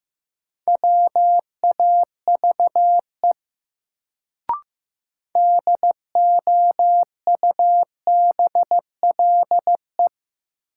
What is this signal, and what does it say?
Morse code, heard as a keyed tone, about 15 wpm, 700 Hz. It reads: WAVE DOUBLE
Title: Sentences from Top 1400 Words - New Word Drill - 15wpm